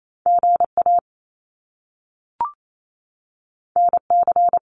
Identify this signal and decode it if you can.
Morse code, heard as a keyed tone, about 28 wpm, 700 Hz. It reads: GA NC
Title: US States 28wpm